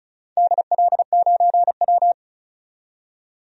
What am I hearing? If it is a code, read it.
Morse code, heard as a keyed tone, about 36 wpm, 700 Hz. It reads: DL9W
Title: DXCC-100 Call Signs - 4 Characters - 35wpm